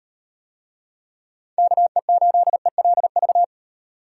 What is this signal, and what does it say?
Morse code, heard as a keyed tone, about 37 wpm, 700 Hz. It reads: KE8ELV